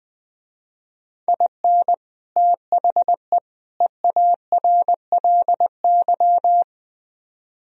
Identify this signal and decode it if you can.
Morse code, heard as a keyed tone, about 23 wpm, 700 Hz. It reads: IN THE EARLY
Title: Speed-Racing - Sets of 3 Words 20wpm